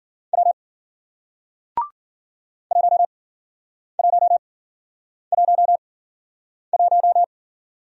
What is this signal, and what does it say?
Morse code, heard as a keyed tone, about 53 wpm, spaced as for 9 wpm, 700 Hz. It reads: W 1111